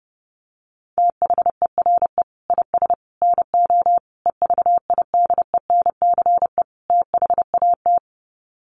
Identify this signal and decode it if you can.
Morse code, heard as a keyed tone, about 30 wpm, 700 Hz. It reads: THERE IS NO EVIDENCE THAT
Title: Sets of 5 Words Encore 30wpm